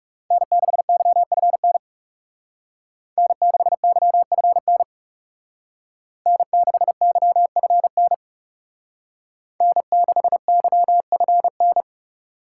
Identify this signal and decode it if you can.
Morse code, heard as a keyed tone, about 38 wpm, 700 Hz. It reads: N6YFN N6YFN N6YFN N6YFN